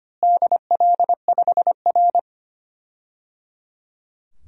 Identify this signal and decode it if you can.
Morse code, heard as a keyed tone, about 25 wpm, 700 Hz. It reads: DL5R
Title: DXCC-100 Call Signs - 4 Characters - Repeated 3x - 25wpm